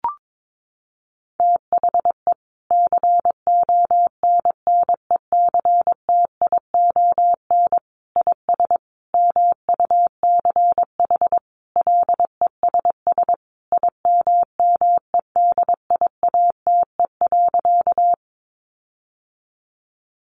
Morse code, heard as a keyed tone, about 22 wpm, 700 Hz. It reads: THE CONNECTION IS MUCH LESS IMMEDIATE.
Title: Sentences from Top 1300 Words 22wpm